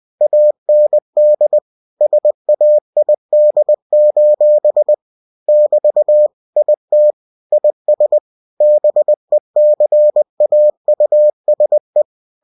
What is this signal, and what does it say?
Morse code, heard as a keyed tone, about 20 wpm, 600 Hz. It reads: AND SAID: <BT> IT IS BECAUSE
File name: cw-books / winnie-the-pooh / winnie-20wpm-page-0062